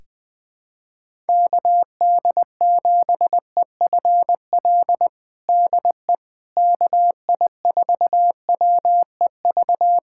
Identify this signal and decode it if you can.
Morse code, heard as a keyed tone, about 20 wpm, 700 Hz. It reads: KD7EFL DE KI4WEV
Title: USA Callsign DE Callsign - 6 Characters - 20wpm